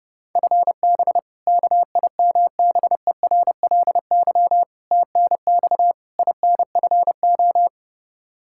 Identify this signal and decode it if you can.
Morse code, heard as a keyed tone, about 30 wpm, 700 Hz. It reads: FB KIMBERLY TNX INFO
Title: Practice QSO - Exchange 3 - More Info - 30wpm